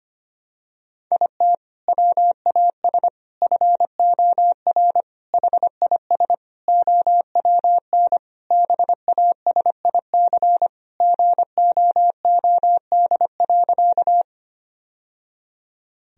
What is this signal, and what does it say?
Morse code, heard as a keyed tone, about 25 wpm, 700 Hz. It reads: IT WAS FOR HIS OWN BASIC GOOD.
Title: Sentences from Top 1000 Words - Review Part 3 - 25wpm